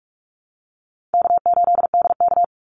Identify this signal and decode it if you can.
Morse code, heard as a keyed tone, about 44 wpm, 700 Hz. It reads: K8DK
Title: Call Signs - 45wpm